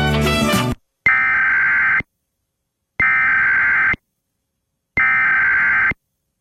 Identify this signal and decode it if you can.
Emergency Alert System (SAME) tones (header bursts), heard as harsh buzzing bursts. It is heard from 1.1 to 5.9 s.